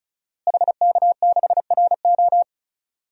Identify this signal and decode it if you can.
Morse code, heard as a keyed tone, about 36 wpm, 700 Hz. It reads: HK6RO
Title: DXCC-100 Call Signs - 5 Characters - Repeated 3x - 35wpm